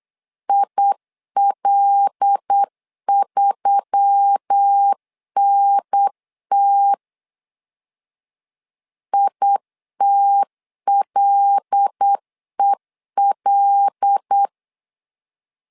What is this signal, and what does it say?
Morse code, heard as a keyed tone, about 8 wpm, 800 Hz. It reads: IL3NT ITLEL
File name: LEZIONI / 5 LBVC / 5 L 8wpm 2831